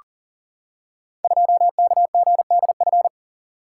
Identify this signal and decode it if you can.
Morse code, heard as a keyed tone, about 39 wpm, 690 Hz. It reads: 2KGDF
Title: Code Groups - 5 Letter:Numbers - 40wpm